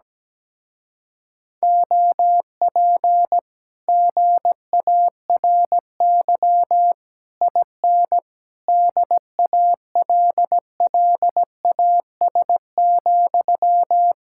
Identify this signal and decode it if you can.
Morse code, heard as a keyed tone, about 17 wpm, 700 Hz. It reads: OP GARY IN DALLAS,
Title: Speed-Racing Common QSO Phrases Encore - 17wpm